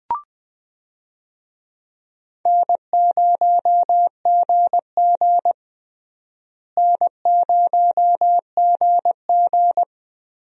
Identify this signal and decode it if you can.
Morse code, heard as a keyed tone, about 20 wpm, 700 Hz. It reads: N0GG N0GG
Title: USA Call Signs - 4 Characters - Repeated 3x - 20wpm